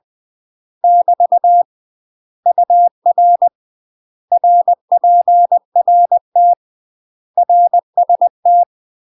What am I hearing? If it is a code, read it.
Morse code, heard as a keyed tone, about 20 wpm, 700 Hz. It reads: <BT> UR RPRT RST
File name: UK MASTER QSOs TRAINING / 20WPM / UK MASTER QSOs 0029